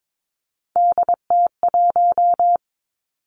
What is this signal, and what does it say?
Morse code, heard as a keyed tone, about 22 wpm, 700 Hz. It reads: DT1
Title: Three Challenging Letter-Number Combinations 22wpm